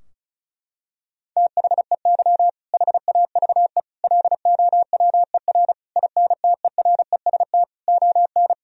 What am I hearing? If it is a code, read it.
Morse code, heard as a keyed tone, about 36 wpm, 700 Hz. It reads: THEY HAVE LOWER INTEREST ON
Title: Sentences from Top 1000 Words - Review Part 4 - 35wpm